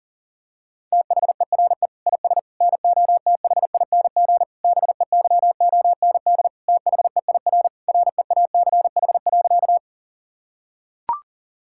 Morse code, heard as a keyed tone, about 40 wpm, 700 Hz. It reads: THERE IS NOTHING BEYOND THEIR REACH.
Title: Sentences from Top 1000 Words - Review Part 6 - 40wpm